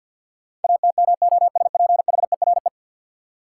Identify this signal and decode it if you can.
Morse code, heard as a keyed tone, about 52 wpm, 700 Hz. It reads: ATMOSPHERE
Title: Sentences from Top 1600 Words - New Word Drill - 50wpm